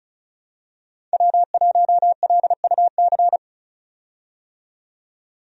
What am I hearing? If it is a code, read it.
Morse code, heard as a keyed tone, about 34 wpm, 700 Hz. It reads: W1LUC
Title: USA Call Signs - 5 Characters - 35wpm